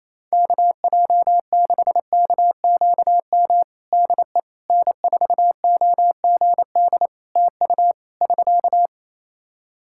Morse code, heard as a keyed tone, about 27 wpm, 700 Hz. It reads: KJ6KQM DE N4OGD TU <SK>
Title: Practice QSO - Exchange 4 - Ending - 28wpm